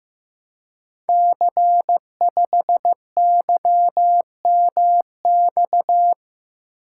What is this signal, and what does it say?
Morse code, heard as a keyed tone, about 15 wpm, 700 Hz. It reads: C5YMX
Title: Code Groups - 5 Letter:Numbers - 15wpm